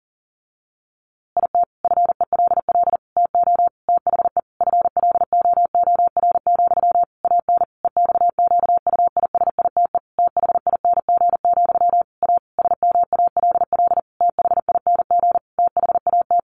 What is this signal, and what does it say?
Morse code, heard as a keyed tone, about 40 wpm, 700 Hz. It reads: IT FELL TO THE FLOOR, AN EXQUISITE THING, A SMALL THING THAT
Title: A Sound Of Thunder 40wpm